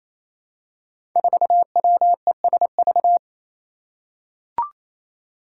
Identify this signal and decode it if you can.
Morse code, heard as a keyed tone, about 27 wpm, 700 Hz. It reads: 4WESV
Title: Five Challenging Letter-Number Combinations 28wpm